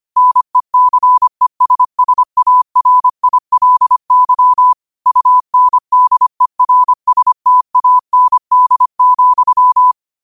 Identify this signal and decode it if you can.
Morse code, heard as a keyed tone, about 25 wpm, 1000 Hz. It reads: NECESSARILY UNDERSTAND,
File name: qotdfiles / 25wpm